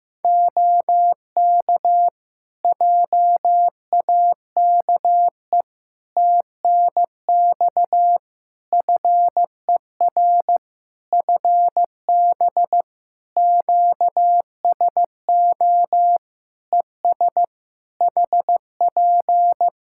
Morse code, heard as a keyed tone, about 15 wpm, 700 Hz. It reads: OK JAKE TNX FER FB QSO ES HP
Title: Practice QSO - Exchange 4 - Ending - 15wpm